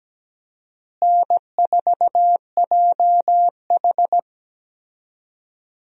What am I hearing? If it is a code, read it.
Morse code, heard as a keyed tone, about 17 wpm, 700 Hz. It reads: N4JH
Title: USA Call Signs - 4 Characters - Repeated 3x - 17wpm